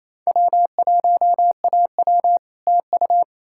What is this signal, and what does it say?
Morse code, heard as a keyed tone, about 37 wpm, 700 Hz. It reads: W1AW TU